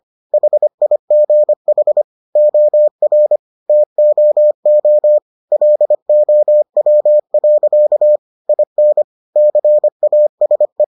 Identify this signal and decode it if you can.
Morse code, heard as a keyed tone, about 25 wpm, 600 Hz. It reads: HIGH OR TOO LOW. IN CASE